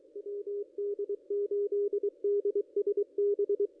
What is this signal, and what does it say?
Morse code, heard as a keyed tone, about 23 wpm, 440 Hz, with radio noise. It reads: WD8DSB
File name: contests / ss-cw-k5tr-op-n5rz -1600